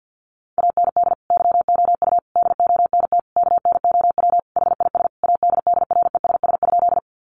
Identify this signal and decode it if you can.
Morse code, heard as a keyed tone, about 49 wpm, 700 Hz. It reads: AND YOU DONT KNOW HIS ADDRESS?
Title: Sentences from Top 1000 Words 50wpm